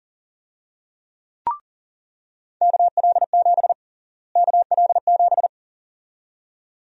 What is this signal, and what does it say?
Morse code, heard as a keyed tone, about 40 wpm, 700 Hz. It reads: KL7 KL7
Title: USA Call Sign Prefixes - Repeated 2x - 40wpm